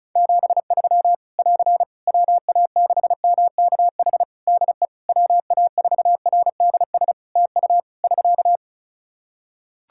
Morse code, heard as a keyed tone, about 36 wpm, 700 Hz. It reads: 73 <AR> WA6MKH DE WA4RDS TU <SK>
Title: Practice QSO - Exchange 4 - Ending - 35wpm